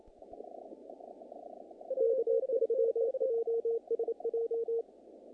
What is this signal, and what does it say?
Morse code, heard as a keyed tone, about 27 wpm, 460 Hz. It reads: K3JSJ
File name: contests / naqp-cw-k5tr-op-k5pi- -2100